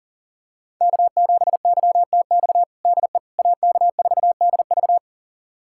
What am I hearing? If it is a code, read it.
Morse code, heard as a keyed tone, about 40 wpm, 700 Hz. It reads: K7YTX DE AK4DV
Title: USA Callsign DE Callsign - 5 Characters - 40wpm